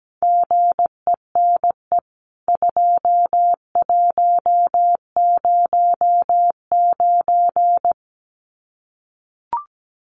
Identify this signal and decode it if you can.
Morse code, heard as a keyed tone, about 17 wpm, 700 Hz. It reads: GENE 2109
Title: CWT Exchange 17wpm